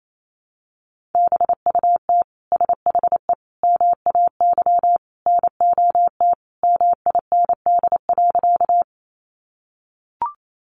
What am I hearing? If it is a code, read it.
Morse code, heard as a keyed tone, about 28 wpm, 700 Hz. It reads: BUT SHE MAY NOT MIND.
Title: Sentences from Top 600 Words 28wpm